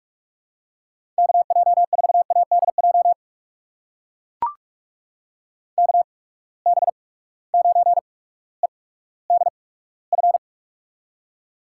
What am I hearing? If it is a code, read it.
Morse code, heard as a keyed tone, about 44 wpm, spaced as for 19 wpm, 700 Hz. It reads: KJ4ADJ KB9EDF
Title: USA Call Signs - 6 Characters - 8x Character Spacing - 45wpm